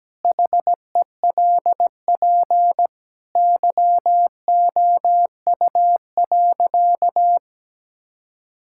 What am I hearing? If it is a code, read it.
Morse code, heard as a keyed tone, about 17 wpm, 700 Hz. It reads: HELP YOU.
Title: Sentences from Top 1000 Words - Review Part 6 - 17wpm